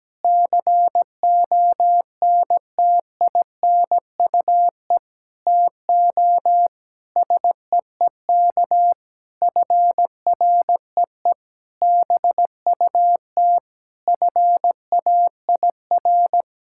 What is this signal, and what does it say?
Morse code, heard as a keyed tone, about 17 wpm, 700 Hz. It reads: CONTINUE TO SEEK FREE BUT FAIR
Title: Sentences from Top 1600 Words 17wpm